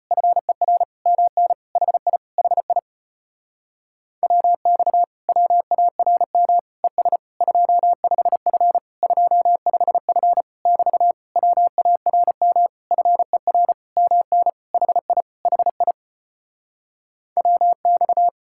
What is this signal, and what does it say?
Morse code, heard as a keyed tone, about 37 wpm, 700 Hz. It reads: FER MN HI HI WX WARM ES 25F 25F <BT> WARM FER MN HI HI WX